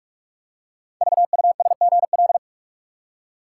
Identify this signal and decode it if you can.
Morse code, heard as a keyed tone, about 44 wpm, 700 Hz. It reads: VUSGL
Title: Code Groups - 5 Letters - 45wpm